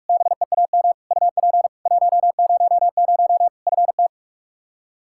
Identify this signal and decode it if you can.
Morse code, heard as a keyed tone, about 46 wpm, 700 Hz. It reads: BEAM UP 100 FT
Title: Practice QSO - Exchange 2 - Info - 45wpm